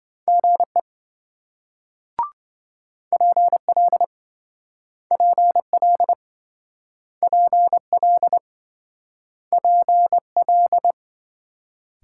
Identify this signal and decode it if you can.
Morse code, heard as a keyed tone, about 26 wpm, 700 Hz. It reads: GE PL PL PL PL